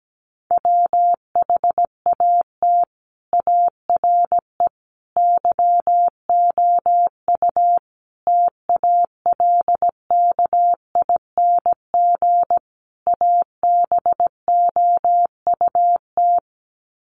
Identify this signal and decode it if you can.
Morse code, heard as a keyed tone, about 17 wpm, 700 Hz. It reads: WHAT ARE YOU TALKING ABOUT
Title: Sets of 5 Words 17wpm